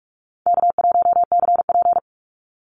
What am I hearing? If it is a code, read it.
Morse code, heard as a keyed tone, about 44 wpm, 700 Hz. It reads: K1CP
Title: Call Signs - 45wpm